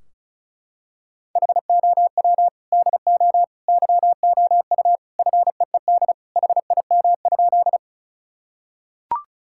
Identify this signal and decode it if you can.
Morse code, heard as a keyed tone, about 36 wpm, 700 Hz. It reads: HOW DO YOU FEED HIM?
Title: Sentences from Top 1600 Words 35wpm